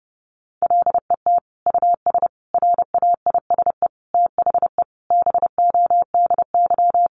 Morse code, heard as a keyed tone, about 30 wpm, 700 Hz. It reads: LET US RAISE THE BODY
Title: Sentences from Top 1400 Words 30wpm